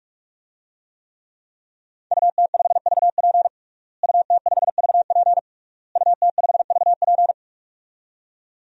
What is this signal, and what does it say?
Morse code, heard as a keyed tone, about 41 wpm, 700 Hz. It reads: UT5VP UT5VP UT5VP
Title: DXCC-100 Call Signs - 5 Characters - Repeated 3x - 45wpm